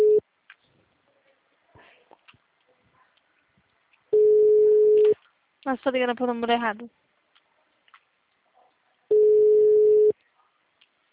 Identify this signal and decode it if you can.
Ringback tone (Europe, 425 Hz), with speech.